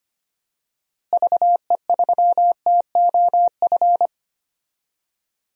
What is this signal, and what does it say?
Morse code, heard as a keyed tone, about 25 wpm, 700 Hz. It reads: VE3TOF
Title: DXCC-100 Call Signs - 6 Characters - Repeated 3x - 25wpm